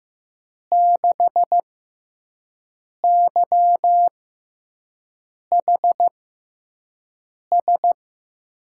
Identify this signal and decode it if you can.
Morse code, heard as a keyed tone, about 15 wpm, spaced as for 5 wpm, 700 Hz. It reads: 6YHS
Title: USA Call Signs - 5 Characters - 6x Character Spacing - 15wpm